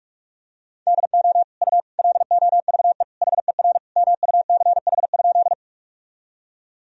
Morse code, heard as a keyed tone, about 43 wpm, 700 Hz. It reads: DO U LOVE HER MUCH?
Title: Sentences from Top 400 Words and Abbreviations 45wpm